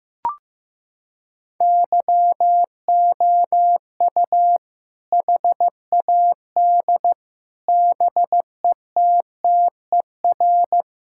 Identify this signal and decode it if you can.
Morse code, heard as a keyed tone, about 14 wpm, 700 Hz. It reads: YOU HAD BETTER
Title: Sentences from Top 1600 Words 15wpm